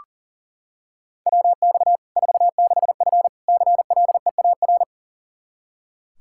Morse code, heard as a keyed tone, about 38 wpm, 700 Hz. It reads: WX 46F CLEAR
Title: Practice QSO - Exchange 2 - Info - 40wpm